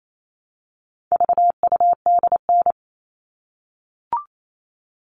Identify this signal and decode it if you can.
Morse code, heard as a keyed tone, about 28 wpm, 700 Hz. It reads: VUDN
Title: All Cut Numbers - 4-Digit 28wpm